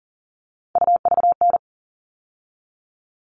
Morse code, heard as a keyed tone, about 40 wpm, 700 Hz. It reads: UVN
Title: All Cut Numbers - 3-Digit 40wpm